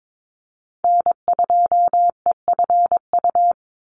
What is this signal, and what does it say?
Morse code, heard as a keyed tone, about 22 wpm, 700 Hz. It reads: N2EFU
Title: USA Call Signs 22wpm